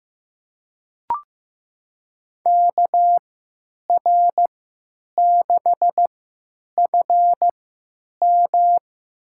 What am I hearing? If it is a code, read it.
Morse code, heard as a keyed tone, about 15 wpm, spaced as for 8 wpm, 700 Hz. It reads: KR6FM
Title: USA Call Signs - 5 Characters - 3x Character Spacing - 15wpm